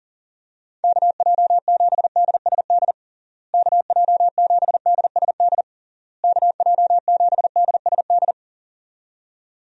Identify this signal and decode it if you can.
Morse code, heard as a keyed tone, about 40 wpm, 700 Hz. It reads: KJ7DSD KJ7DSD KJ7DSD
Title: USA Call Signs - 6 Characters - Repeated 3x - 40wpm